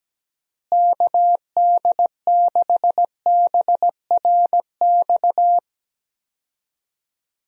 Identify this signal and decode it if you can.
Morse code, heard as a keyed tone, about 17 wpm, 700 Hz. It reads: KD6BRX
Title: USA Call Signs - 6 Characters - Repeated 3x - 17wpm